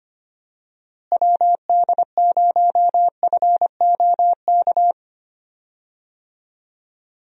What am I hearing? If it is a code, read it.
Morse code, heard as a keyed tone, about 25 wpm, 700 Hz. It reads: WD0FOK
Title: USA Call Signs - 6 Characters - Repeated 3x - 25wpm